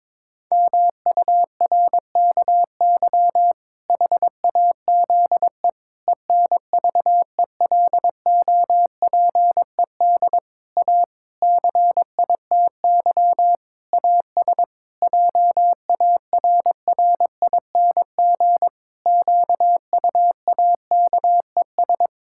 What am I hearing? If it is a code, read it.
Morse code, heard as a keyed tone, about 22 wpm, 700 Hz. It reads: MURKY HAZE ENVELOPED A CITY AS JARRING QUAKES
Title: Pangram Challenge v2 - 22wpm